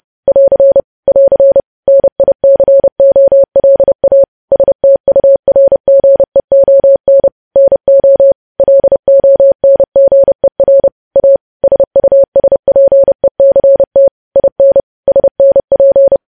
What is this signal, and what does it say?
Morse code, heard as a keyed tone, about 30 wpm, 550 Hz. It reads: <AR> <AR> NICOLA STURGEON NO LONGER A SUSPECT IN SNP
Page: news headlines in morse code at 30 wpm